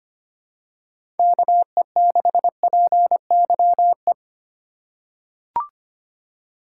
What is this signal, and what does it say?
Morse code, heard as a keyed tone, about 25 wpm, 700 Hz. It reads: KE6PYE
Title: USA Call Signs - 2x Character Spacing - 25wpm